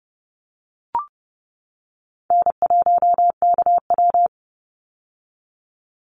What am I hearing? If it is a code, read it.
Morse code, heard as a keyed tone, about 30 wpm, 700 Hz. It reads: N1KW
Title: Easy Call Signs - 30wpm